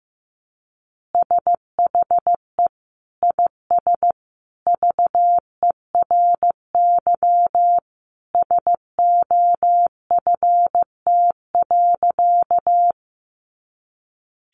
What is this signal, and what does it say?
Morse code, heard as a keyed tone, about 15 wpm, 700 Hz. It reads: SHE IS VERY SOFT.
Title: Easy Phrases 15wpm